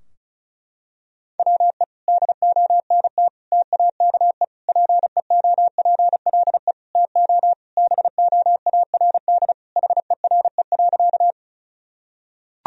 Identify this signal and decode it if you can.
Morse code, heard as a keyed tone, about 36 wpm, 700 Hz. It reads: WE DONT TAKE PEOPLE TO BOARD HERE.
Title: Sentences from Top 1000 Words - Review Part 4 - 35wpm